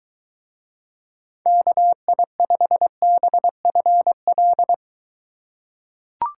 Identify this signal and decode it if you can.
Morse code, heard as a keyed tone, about 22 wpm, 700 Hz. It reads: KI5BFL